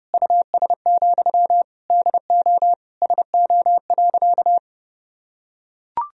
Morse code, heard as a keyed tone, about 32 wpm, 700 Hz. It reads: US, DO SO.
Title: Sentences from Top 1000 Words - Review Part 2 - 30wpm